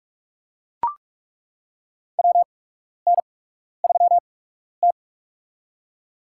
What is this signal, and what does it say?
Morse code, heard as a keyed tone, about 43 wpm, spaced as for 12 wpm, 700 Hz. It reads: WN3T
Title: USA Call Signs - 4 Characters - 8x Character Spacing - 45wpm